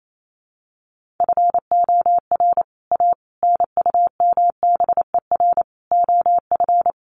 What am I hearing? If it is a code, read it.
Morse code, heard as a keyed tone, about 28 wpm, 700 Hz. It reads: FOR A NUMBER OF
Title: Sets of 5 Words Encore2 28wpm